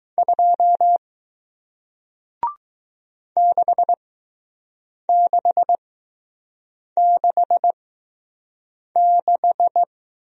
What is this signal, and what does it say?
Morse code, heard as a keyed tone, about 20 wpm, spaced as for 6 wpm, 700 Hz. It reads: 2 6666